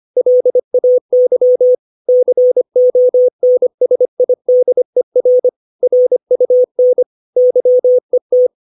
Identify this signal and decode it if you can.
Morse code, heard as a keyed tone, about 25 wpm, 500 Hz. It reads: LAY CONSIDER RUN YET
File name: 632 words / 632 Words 25WPM